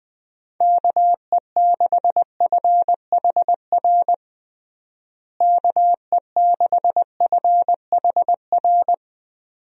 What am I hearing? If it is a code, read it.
Morse code, heard as a keyed tone, about 21 wpm, 700 Hz. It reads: KE6FHR KE6FHR
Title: USA Call Signs - 6 Characters - Repeated 2x - 20wpm